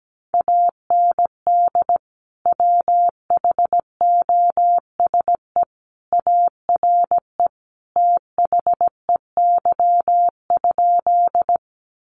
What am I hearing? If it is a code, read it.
Morse code, heard as a keyed tone, about 17 wpm, 700 Hz. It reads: AND WHOSE ARE THEY?
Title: Sentences from Top 1200 Words 17wpm